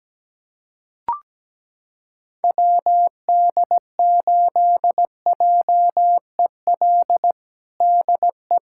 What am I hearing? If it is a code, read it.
Morse code, heard as a keyed tone, about 17 wpm, 700 Hz. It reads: WD8JEL DE
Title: USA Callsign DE Callsign - 6 Characters - 17wpm